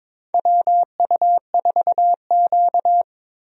Morse code, heard as a keyed tone, about 23 wpm, 700 Hz. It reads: WU4Q
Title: USA Call Signs - 4 Characters - 22wpm